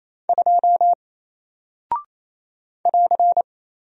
Morse code, heard as a keyed tone, about 27 wpm, 700 Hz. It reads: 2 <AR>
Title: Single Character (Letter:Number:Punctuation:Prosign) 28wpm